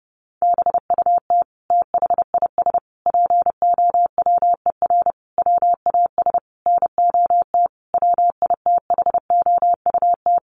Morse code, heard as a keyed tone, about 30 wpm, 700 Hz. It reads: BUT THIS POWER WAS NOT WITHOUT
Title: Sentences from Top 1500 Words 30wpm